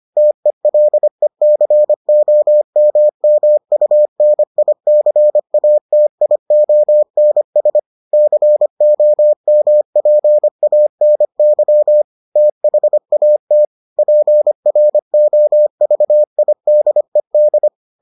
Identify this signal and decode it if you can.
Morse code, heard as a keyed tone, about 25 wpm, 600 Hz. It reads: TELECOMMUNICATIONS COMPANY THAT PROVIDED